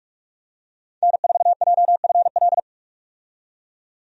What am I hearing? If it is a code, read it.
Morse code, heard as a keyed tone, about 47 wpm, 700 Hz. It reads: N4JFL
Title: USA Call Signs - 5 Characters - 45wpm